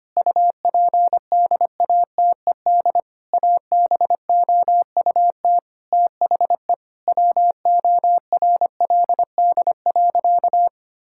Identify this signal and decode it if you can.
Morse code, heard as a keyed tone, about 25 wpm, 700 Hz. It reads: UPDATED ABOUT THE WORLD.
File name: practice / 25wpm / Difficult Sentences with Hints - 25wpm